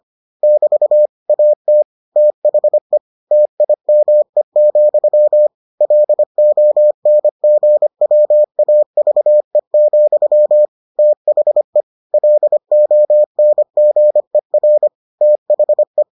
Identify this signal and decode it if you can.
Morse code, heard as a keyed tone, about 25 wpm, 600 Hz. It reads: <BT> AT THE TIME, LONGWAVE, THE LONGER THE